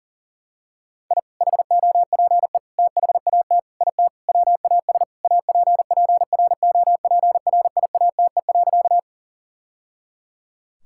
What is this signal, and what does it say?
Morse code, heard as a keyed tone, about 40 wpm, 700 Hz. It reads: I HOPE THAT IT WAS APPROPRIATE.
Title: Sentences from Top 1000 Words - Review Part 5 - 40wpm